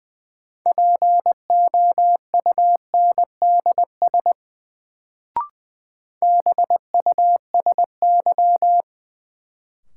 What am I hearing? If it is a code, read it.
Morse code, heard as a keyed tone, about 20 wpm, 700 Hz. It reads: POUNDS BUSY
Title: Sentences from Top 1000 Words - Review Part 2 - New Word Drill - 20wpm